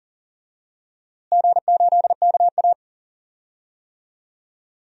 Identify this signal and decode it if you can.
Morse code, heard as a keyed tone, about 40 wpm, 700 Hz. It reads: G8KA
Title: DXCC-100 Call Signs - 4 Characters - Repeated 2x - 40wpm